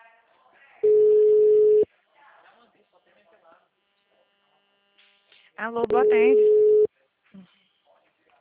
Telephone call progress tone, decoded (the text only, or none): ringback tone